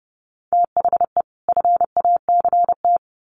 Morse code, heard as a keyed tone, about 30 wpm, 700 Hz. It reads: THE FACT
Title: Sets of 2 Words 30wpm